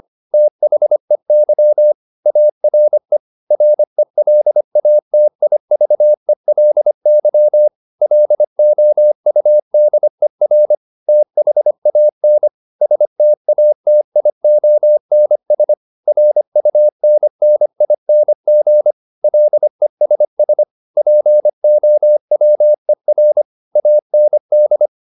Morse code, heard as a keyed tone, about 25 wpm, 600 Hz. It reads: THEY ARE RELATIVELY LOUDER THAN STATIONS RUNNING LESS POWER AND